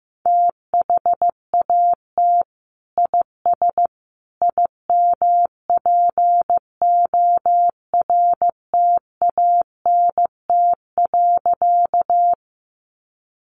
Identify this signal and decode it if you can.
Morse code, heard as a keyed tone, about 15 wpm, 700 Hz. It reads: THAT IS IMPORTANT.
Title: Sentences from Top 600 Words 15wpm